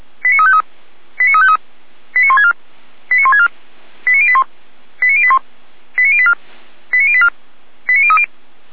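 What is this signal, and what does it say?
Five-tone selective call (ZVEI), heard as quick runs of five tones, tones of about 70 ms. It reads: 89353 89353 89164 89164 89081 89081 89084 89084 89030
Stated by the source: ZVEI five-tone call received by SDR.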